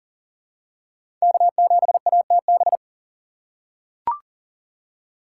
Morse code, heard as a keyed tone, about 40 wpm, 700 Hz. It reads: K7ATB
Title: USA Call Signs - 5 Characters - 4x Character Spacing - 40wpm